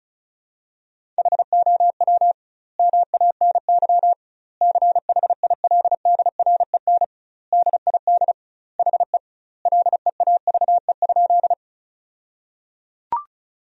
Morse code, heard as a keyed tone, about 36 wpm, 700 Hz. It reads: HOW MANY CHILDREN DID HE LEAVE?
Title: 2x Word-Spacing - Sentences from Top 400 Words - 35wpm